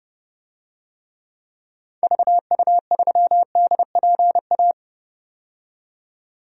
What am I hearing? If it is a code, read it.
Morse code, heard as a keyed tone, about 30 wpm, 700 Hz. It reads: VU3DPA
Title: DXCC-100 Call Signs - 6 Characters - 30wpm